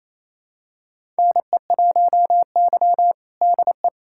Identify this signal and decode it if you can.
Morse code, heard as a keyed tone, about 28 wpm, 700 Hz. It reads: NE1Y DE
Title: USA Callsign DE Callsign 28wpm